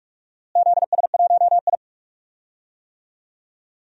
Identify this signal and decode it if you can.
Morse code, heard as a keyed tone, about 46 wpm, 700 Hz. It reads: ZS1I
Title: DXCC-100 Call Signs - 4 Characters - Repeated 2x - 45wpm